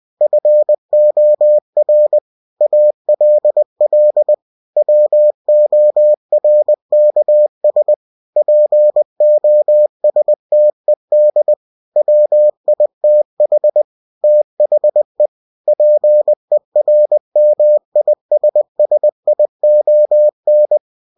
Morse code, heard as a keyed tone, about 20 wpm, 600 Hz. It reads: FOR ALL WORKS POSTED WITH THE PERMISSION
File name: cw-books / winnie-the-pooh / winnie-20wpm-page-0115